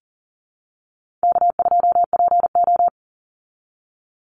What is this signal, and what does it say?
Morse code, heard as a keyed tone, about 40 wpm, 700 Hz. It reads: K2PO
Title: Easy Call Signs Repeated 2x - 40wpm